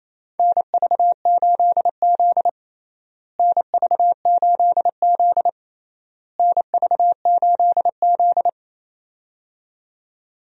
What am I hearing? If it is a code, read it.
Morse code, heard as a keyed tone, about 28 wpm, 700 Hz. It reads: NV8Z NV8Z NV8Z
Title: USA Call Signs - 4 Characters - Repeated 3x - 28wpm